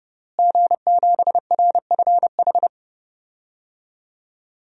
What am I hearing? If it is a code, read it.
Morse code, heard as a keyed tone, about 31 wpm, 700 Hz. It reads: G7RFH
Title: DXCC-100 Call Signs - 5 Characters - Repeated 2x - 30wpm